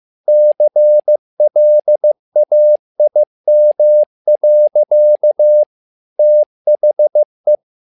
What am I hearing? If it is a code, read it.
Morse code, heard as a keyed tone, about 15 wpm, 600 Hz. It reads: CLAIM. THE
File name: learning center / tower trouble / tower trouble 15wpm